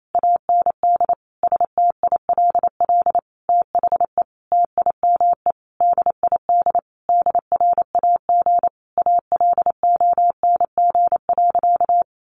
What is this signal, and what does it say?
Morse code, heard as a keyed tone, about 28 wpm, 700 Hz. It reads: AND STILL THE TIME DID DRAG ALONG.
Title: Sentences from Top 1500 Words 28wpm